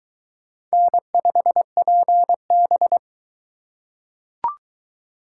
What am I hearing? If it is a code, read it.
Morse code, heard as a keyed tone, about 19 wpm, 700 Hz. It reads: N5PB